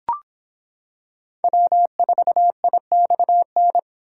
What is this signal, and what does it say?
Morse code, heard as a keyed tone, about 26 wpm, 700 Hz. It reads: W4IXN